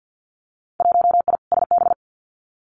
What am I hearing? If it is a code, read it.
Morse code, heard as a keyed tone, about 50 wpm, 700 Hz. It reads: 1I SB
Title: Winter Field Day Exchanges Encore 50wpm